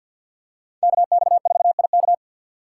Morse code, heard as a keyed tone, about 51 wpm, 700 Hz. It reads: KX4IK
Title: USA Call Signs - 5 Characters - 3x Character Spacing - 50wpm